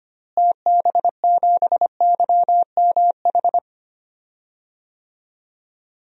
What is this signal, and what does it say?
Morse code, heard as a keyed tone, about 25 wpm, 690 Hz. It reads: TB7YMH
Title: DXCC-100 Call Signs - 6 Characters - Repeated 3x - 25wpm